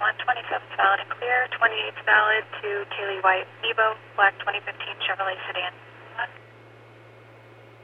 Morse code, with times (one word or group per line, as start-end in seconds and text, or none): none